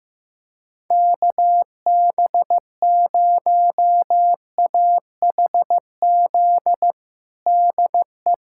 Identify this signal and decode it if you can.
Morse code, heard as a keyed tone, about 15 wpm, 700 Hz. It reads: KB0AHZ DE
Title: USA Callsign DE Callsign - 6 Characters - 15wpm